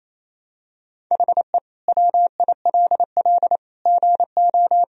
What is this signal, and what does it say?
Morse code, heard as a keyed tone, about 27 wpm, 700 Hz. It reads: HE WILL GO
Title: Sentences from Top 1000 Words - Review Part 2 - 28wpm